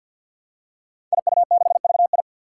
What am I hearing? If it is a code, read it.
Morse code, heard as a keyed tone, about 46 wpm, 700 Hz. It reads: IU6VI
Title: Five Challenging Letter-Number Combinations 50wpm